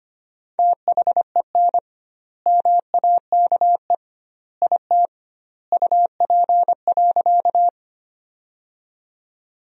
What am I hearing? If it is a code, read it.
Morse code, heard as a keyed tone, about 25 wpm, 700 Hz. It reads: THEN MAKE IT UP.
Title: 2x Word-Spacing - Sentences from Top 100 Words - 25wpm